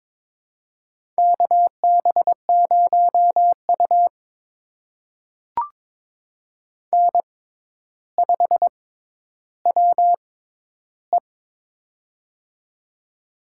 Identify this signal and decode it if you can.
Morse code, heard as a keyed tone, about 22 wpm, spaced as for 11 wpm, 700 Hz. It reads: KB0U N5WE
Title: USA Call Signs - 4 Characters - 6x Character Spacing - 22wpm